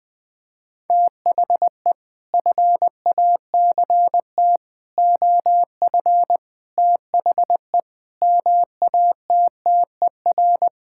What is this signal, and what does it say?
Morse code, heard as a keyed tone, about 20 wpm, 700 Hz. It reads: THE FACT OF THE MATTER
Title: Sentences with 6-Word Phrases - Spoken Beginning - 20wpm